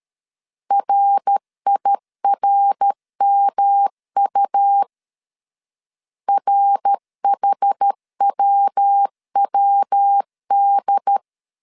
Morse code, heard as a keyed tone, about 13 wpm, 800 Hz. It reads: RIRMU RHWWD
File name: CW / DET / 1 3 1500C S B 12wpm 2828